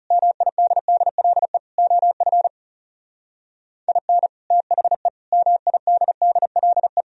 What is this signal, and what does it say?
Morse code, heard as a keyed tone, about 38 wpm, 700 Hz. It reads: MIDDLE OF IN THE MIDDLE
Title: Speed-Racing - Sets of 4 Words 30wpm